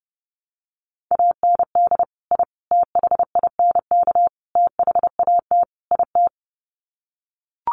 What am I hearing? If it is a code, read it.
Morse code, heard as a keyed tone, about 30 wpm, 700 Hz. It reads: AND I THINK THAT IT
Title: Sets of 5 Words Encore2 30wpm